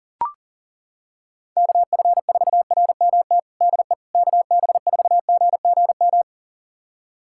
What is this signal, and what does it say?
Morse code, heard as a keyed tone, about 37 wpm, 700 Hz. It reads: KF4RMT DE KB4GGM
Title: USA Callsign DE Callsign - 6 Characters - 40wpm